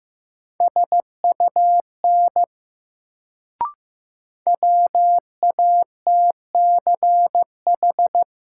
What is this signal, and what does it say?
Morse code, heard as a keyed tone, about 15 wpm, 700 Hz. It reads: SUN WATCH
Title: Sentences from Top 300 Words - New Word Drill - 15wpm